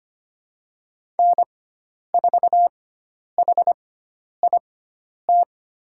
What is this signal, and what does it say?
Morse code, heard as a keyed tone, about 25 wpm, spaced as for 10 wpm, 700 Hz. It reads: N4HIT
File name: Lesson 16g - Letter c - Call Signs 25wpm